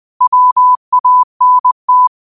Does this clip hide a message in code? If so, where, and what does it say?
Yes: Morse code from 0.2 to 2.1 s, heard as a keyed tone, about 20 wpm, 1000 Hz. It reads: WANT